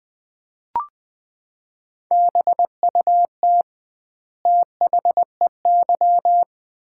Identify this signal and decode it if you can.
Morse code, heard as a keyed tone, about 20 wpm, 700 Hz. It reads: BUT THEY
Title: 2x Word-Spacing - Sets of 2 Words - 20wpm